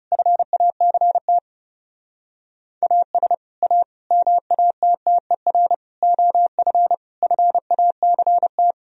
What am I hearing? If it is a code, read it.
Morse code, heard as a keyed tone, about 36 wpm, 700 Hz. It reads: FACT AS A MATTER OF FACT
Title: Speed-Racing - Sets of 5 Words 30wpm